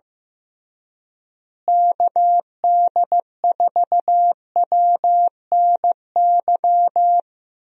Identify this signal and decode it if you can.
Morse code, heard as a keyed tone, about 15 wpm, 700 Hz. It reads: KD4WNY
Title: USA Call Signs - 6 Characters - Repeated 2x - 15wpm